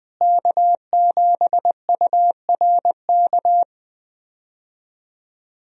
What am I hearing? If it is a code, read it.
Morse code, heard as a keyed tone, about 20 wpm, 700 Hz. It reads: K7URK
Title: USA Call Signs - 5 Characters - Repeated 2x - 20wpm